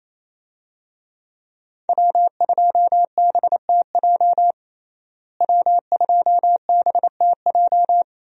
Morse code, heard as a keyed tone, about 28 wpm, 700 Hz. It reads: W2BTJ W2BTJ
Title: USA Call Signs - 5 Characters - Repeated 2x - 28wpm